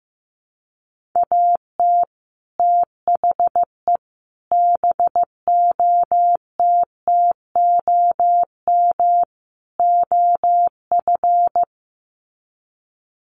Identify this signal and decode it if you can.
Morse code, heard as a keyed tone, about 15 wpm, 700 Hz. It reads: AT THE BOTTOM OF
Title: Sets of 4 Words 15wpm